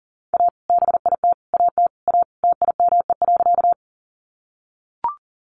Morse code, heard as a keyed tone, about 40 wpm, 700 Hz. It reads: A BIT AT A TIME.
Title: Sentences from Top 900 Words 40wpm